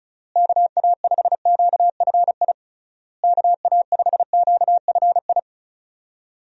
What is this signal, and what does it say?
Morse code, heard as a keyed tone, about 36 wpm, 700 Hz. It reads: KA5QFI KA5QFI
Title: USA Call Signs - 6 Characters - Repeated 3x - 35wpm